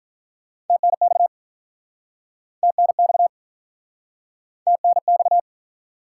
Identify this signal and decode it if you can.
Morse code, heard as a keyed tone, about 51 wpm, 700 Hz. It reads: TNX TNX TNX